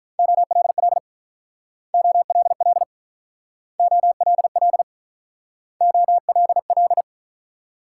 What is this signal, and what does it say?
Morse code, heard as a keyed tone, about 47 wpm, 700 Hz. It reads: OLL OLL OLL OLL